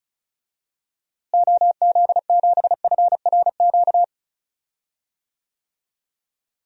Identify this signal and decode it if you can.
Morse code, heard as a keyed tone, about 34 wpm, 700 Hz. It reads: OZ7FRQ
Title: DXCC-100 Call Signs - 6 Characters - Repeated 3x - 35wpm